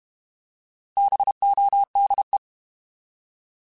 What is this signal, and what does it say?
Morse code, heard as a keyed tone, about 32 wpm, 800 Hz. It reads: BODE
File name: Alphabetical / Alphabetical mp3s / Bs @30